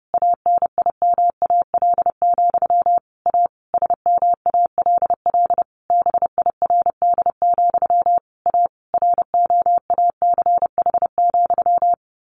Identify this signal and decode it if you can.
Morse code, heard as a keyed tone, about 30 wpm, 700 Hz. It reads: ANIMAL, A SMALL BIRD, A ROACH,
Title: A Sound Of Thunder 30wpm